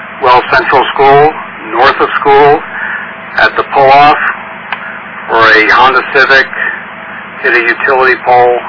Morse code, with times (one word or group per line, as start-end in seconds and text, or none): none